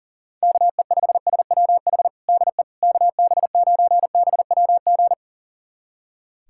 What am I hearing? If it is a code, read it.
Morse code, heard as a keyed tone, about 40 wpm, 700 Hz. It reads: KE5SWH DE KB9BWG
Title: USA Callsign DE Callsign - 6 Characters - 40wpm